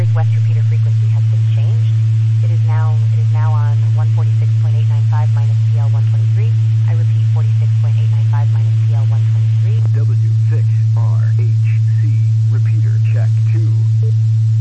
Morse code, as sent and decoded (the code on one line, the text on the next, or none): none